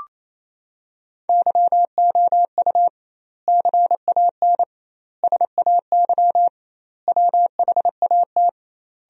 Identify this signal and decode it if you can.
Morse code, heard as a keyed tone, about 28 wpm, 700 Hz. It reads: YOU CAN SAY WHAT
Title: 2x Word-Spacing - Sentences from Top 200 Words - 28wpm